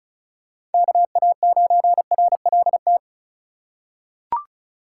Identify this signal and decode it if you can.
Morse code, heard as a keyed tone, about 34 wpm, 700 Hz. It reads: KA9RLT
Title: USA Call Signs - 6 Characters - 3x Character Spacing - 35wpm